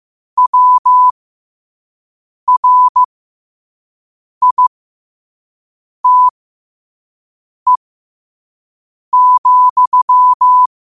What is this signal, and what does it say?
Morse code, heard as a keyed tone, about 15 wpm, spaced as for 5 wpm, 1000 Hz. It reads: WRITE,
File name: qotdfiles / 5wpm